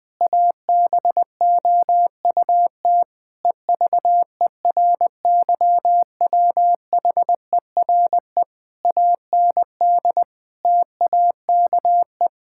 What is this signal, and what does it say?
Morse code, heard as a keyed tone, about 19 wpm, 700 Hz. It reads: ABOUT EVERYWHERE AND TAKE
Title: Sentences from Top 1600 Words 20wpm